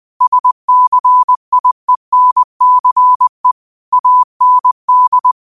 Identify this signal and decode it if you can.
Morse code, heard as a keyed tone, about 20 wpm, 1000 Hz. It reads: SCIENCE AND
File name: qotdfiles / 20wpm